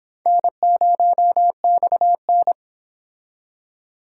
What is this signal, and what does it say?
Morse code, heard as a keyed tone, about 25 wpm, 700 Hz. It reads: N0XN